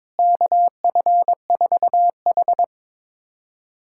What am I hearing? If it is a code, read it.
Morse code, heard as a keyed tone, about 22 wpm, 700 Hz. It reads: KF4H
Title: USA Callsign DE Callsign - 4 Characters - 22wpm